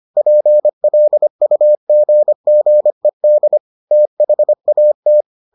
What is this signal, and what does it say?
Morse code, heard as a keyed tone, about 25 wpm, 600 Hz. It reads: PLUGGED THAT